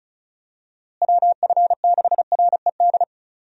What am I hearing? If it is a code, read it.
Morse code, heard as a keyed tone, about 36 wpm, 700 Hz. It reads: WF6RED
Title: USA Call Signs - 6 Characters - Repeated 3x - 35wpm